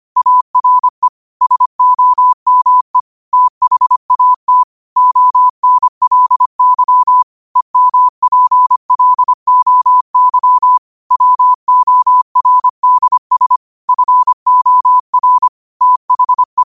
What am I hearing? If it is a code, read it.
Morse code, heard as a keyed tone, about 25 wpm, 1000 Hz. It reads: ARE SOME THAT ONLY EMPLOY WORDS FOR THE
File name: qotdfiles / 25wpm